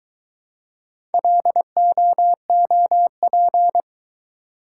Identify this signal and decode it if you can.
Morse code, heard as a keyed tone, about 22 wpm, 700 Hz. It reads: LOOP